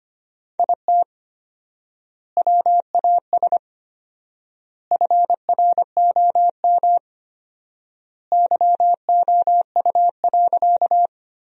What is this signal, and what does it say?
Morse code, heard as a keyed tone, about 25 wpm, 700 Hz. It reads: IT WAS FROM YOU.
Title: 4x Word-Spacing - Sentences from Top 100 Words - 25wpm